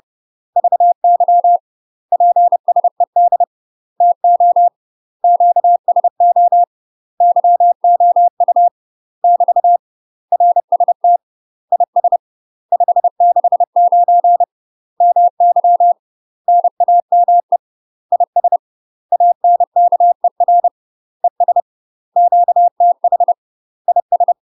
Morse code, heard as a keyed tone, about 30 wpm, 700 Hz. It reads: VY PSED TO QSO YOU <BT> RST IS 569 MY NAME IS ANKER ES QTH IS
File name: UK MASTER QSOs TRAINING / 30WPM / UK MASTER QSOs 0046